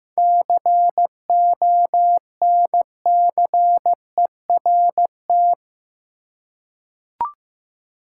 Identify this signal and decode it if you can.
Morse code, heard as a keyed tone, about 15 wpm, 700 Hz. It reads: CONCERT
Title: Seven Letter Words - 15wpm